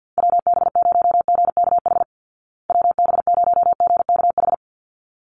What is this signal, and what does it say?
Morse code, heard as a keyed tone, about 50 wpm, 700 Hz. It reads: WB0GKH WB0GKH
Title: Call Signs Repeated 3x - 50wpm